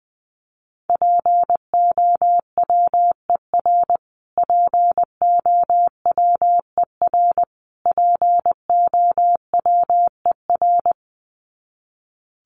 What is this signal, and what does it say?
Morse code, heard as a keyed tone, about 20 wpm, 700 Hz. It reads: POWER POWER POWER
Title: Top 500 Words Repeated 20wpm